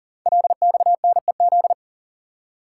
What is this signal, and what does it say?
Morse code, heard as a keyed tone, about 39 wpm, 700 Hz. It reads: LXNEZ
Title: Code Groups - 5 Letters - 40wpm